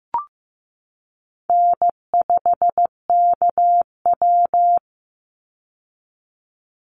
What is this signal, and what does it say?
Morse code, heard as a keyed tone, about 15 wpm, 700 Hz. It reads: N5KW
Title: Call Signs Repeated 3x - 15wpm